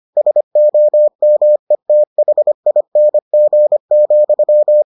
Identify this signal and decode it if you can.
Morse code, heard as a keyed tone, about 25 wpm, 600 Hz. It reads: SOMETHING,